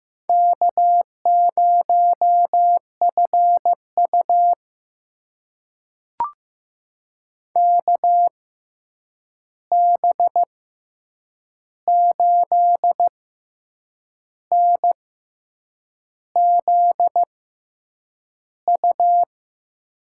Morse code, heard as a keyed tone, about 15 wpm, spaced as for 7 wpm, 700 Hz. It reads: K0FU KB8NZU
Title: USA Call Signs - 6x Character Spacing - 15wpm